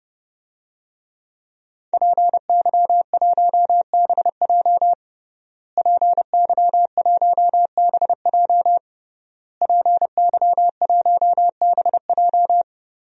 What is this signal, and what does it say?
Morse code, heard as a keyed tone, about 32 wpm, 700 Hz. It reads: PY1BJ PY1BJ PY1BJ
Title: DXCC-100 Call Signs - 5 Characters - Repeated 3x - 30wpm